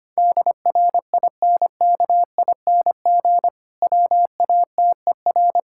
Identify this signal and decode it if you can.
Morse code, heard as a keyed tone, about 25 wpm, 700 Hz. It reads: DRINKING WATER
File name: practice / 25wpm / Difficult Sentences with Hints - 25wpm